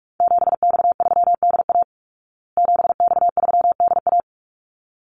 Morse code, heard as a keyed tone, about 44 wpm, 700 Hz. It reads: 7X3DA 7X3DA
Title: Call Signs Repeated 3x - 45wpm